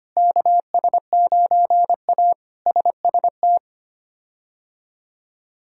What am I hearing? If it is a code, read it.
Morse code, heard as a keyed tone, about 25 wpm, 700 Hz. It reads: KS9A SST
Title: K1USN SST QSO Practice - 25wpm